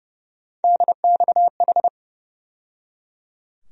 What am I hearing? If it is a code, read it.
Morse code, heard as a keyed tone, about 30 wpm, 700 Hz. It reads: DXH
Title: USA Call Sign Postfixes - 3 Letters 30wpm